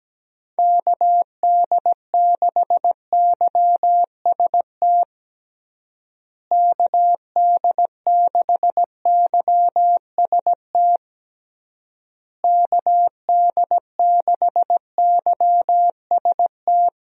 Morse code, heard as a keyed tone, about 17 wpm, 700 Hz. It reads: KD6YST KD6YST KD6YST
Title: USA Call Signs - 6 Characters - Repeated 3x - 17wpm